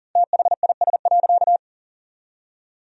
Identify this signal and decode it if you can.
Morse code, heard as a keyed tone, about 40 wpm, 700 Hz. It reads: THIS.
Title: 4x Word-Spacing - Sentences from Top 200 Words - 40wpm